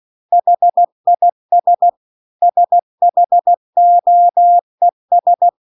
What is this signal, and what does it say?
Morse code, heard as a keyed tone, about 16 wpm, 700 Hz. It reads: HIS SHOES
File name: CW-The-Big-Twoheaded-River-2-16WPM